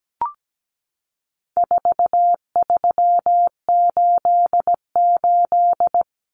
Morse code, heard as a keyed tone, about 17 wpm, 700 Hz. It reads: 4388
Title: Common Cut Numbers - 4-Digit 17wpm